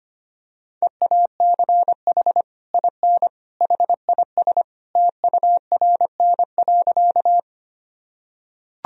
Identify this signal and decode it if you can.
Morse code, heard as a keyed tone, about 25 wpm, 700 Hz. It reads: EACH IN HIS TURN.
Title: Sentences from Top 1000 Words - Review Part 1 - 25wpm